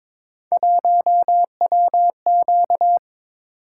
Morse code, heard as a keyed tone, about 23 wpm, 700 Hz. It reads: 1WQ
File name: practice / 22wpm / Three Letter-Number Combinations - 22wpm